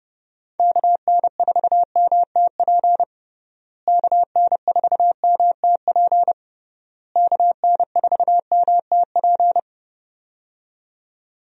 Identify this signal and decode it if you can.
Morse code, heard as a keyed tone, about 30 wpm, 700 Hz. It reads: KN4MTP KN4MTP KN4MTP
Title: USA Call Signs - 6 Characters - Repeated 3x - 30wpm